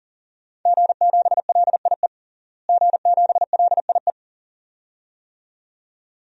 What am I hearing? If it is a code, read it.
Morse code, heard as a keyed tone, about 40 wpm, 700 Hz. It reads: G7LIE G7LIE
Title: DXCC-100 Call Signs - 5 Characters - Repeated 2x - 40wpm